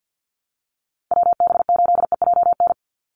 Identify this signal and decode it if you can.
Morse code, heard as a keyed tone, about 49 wpm, 700 Hz. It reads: WB8EJN
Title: Call Signs - 50wpm